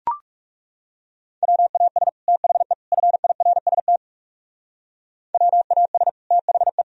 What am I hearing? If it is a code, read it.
Morse code, heard as a keyed tone, about 37 wpm, 700 Hz. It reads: WAS THE FIRST WAS THE
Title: Speed-Racing - Sets of 3 Words 30wpm